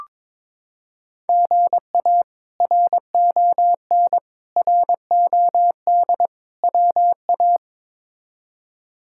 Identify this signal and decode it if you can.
Morse code, heard as a keyed tone, about 22 wpm, 700 Hz. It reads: GA RON ROD WA
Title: K1USN SST QSO Practice - 22wpm